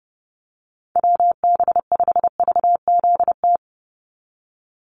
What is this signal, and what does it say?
Morse code, heard as a keyed tone, about 30 wpm, 700 Hz. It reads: WB5VZT
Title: USA Call Signs 30wpm